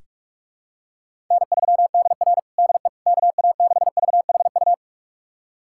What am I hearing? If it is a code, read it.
Morse code, heard as a keyed tone, about 46 wpm, 700 Hz. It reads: N3DR DE KA6VHU
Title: USA Callsign DE Callsign 45wpm